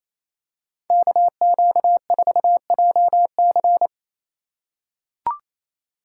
Morse code, heard as a keyed tone, about 27 wpm, 700 Hz. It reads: KQ4JC
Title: USA Call Signs - 4x Character Spacing - 28wpm